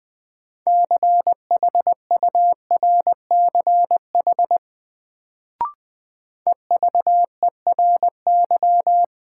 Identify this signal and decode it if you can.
Morse code, heard as a keyed tone, about 20 wpm, 700 Hz. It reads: CHURCH EVERY
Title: Sets of 5 Words - New Word Drill - 20wpm